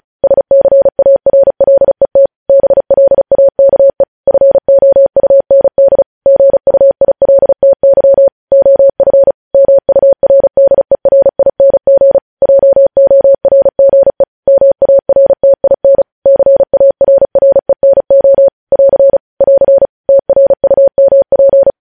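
Morse code, heard as a keyed tone, about 36 wpm, 550 Hz. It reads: SCARLET BLAKE FOUND GUILTY OF MURDERING JORGE MARTIN CARRENO <AR> <AR> TRUMP
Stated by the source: News headlines sent in Morse at 35 WPM.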